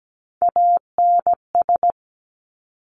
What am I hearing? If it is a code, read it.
Morse code, heard as a keyed tone, about 17 wpm, 700 Hz. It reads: ANS
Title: Top Three Letter Combinations in Words 17wpm